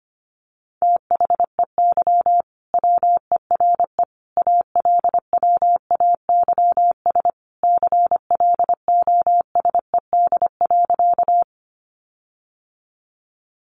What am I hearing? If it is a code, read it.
Morse code, heard as a keyed tone, about 25 wpm, 700 Hz. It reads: THEY WERE ALWAYS CLOSED.
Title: Sentences from Top 1400 Words 25wpm